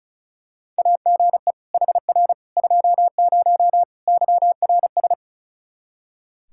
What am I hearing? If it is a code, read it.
Morse code, heard as a keyed tone, about 36 wpm, 700 Hz. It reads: AGE HR 20 YRS
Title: Practice QSO - Exchange 3 - More Info - 35wpm